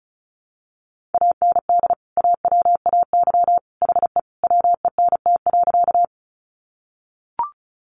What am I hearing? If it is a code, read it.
Morse code, heard as a keyed tone, about 34 wpm, 700 Hz. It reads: AND AWAY HE WENT.
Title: Sentences from Top 600 Words 35wpm